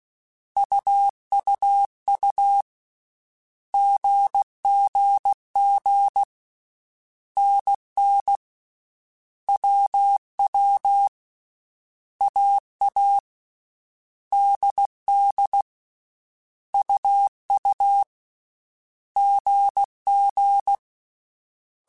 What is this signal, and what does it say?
Morse code, heard as a keyed tone, about 16 wpm, 800 Hz. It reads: UUU GGG NN WW AA DD UU GG
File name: Basic Sets / Basic Letter Groups Collective7 / AWUNDG@13